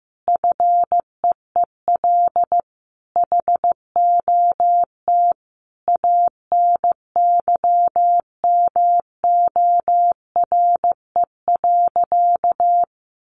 Morse code, heard as a keyed tone, about 15 wpm, 700 Hz. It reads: FEEL HOT ANYMORE.
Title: Additional Easy Phrases 15wpm